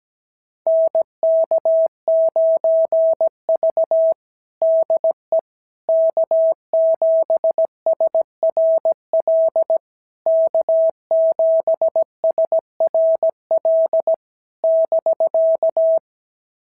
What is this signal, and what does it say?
Morse code, heard as a keyed tone, about 17 wpm, 650 Hz. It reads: NK9V DE K7SRL K7SRL <BK>
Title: SOTA QSO Practice 17wpm